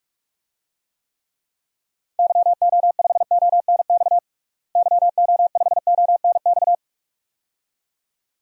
Morse code, heard as a keyed tone, about 46 wpm, 700 Hz. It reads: YO5ONX YO5ONX
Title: DXCC-100 Call Signs - 6 Characters - Repeated 2x - 45wpm